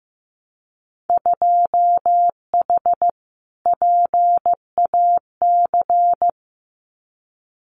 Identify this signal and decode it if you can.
Morse code, heard as a keyed tone, about 15 wpm, 700 Hz. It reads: 2H PAC
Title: Winter Field Day Exchanges Encore 15wpm